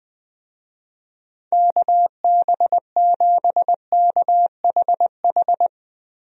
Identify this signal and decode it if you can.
Morse code, heard as a keyed tone, about 20 wpm, 700 Hz. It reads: KB7KHH
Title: USA Call Signs - 6 Characters - Repeated 2x - 20wpm